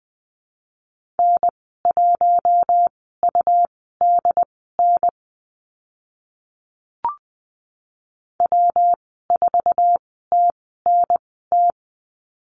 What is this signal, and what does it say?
Morse code, heard as a keyed tone, about 20 wpm, spaced as for 14 wpm, 700 Hz. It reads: N1UDN W4TNT
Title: 161 Lesson 26g - Letter g - Call Signs 20wpm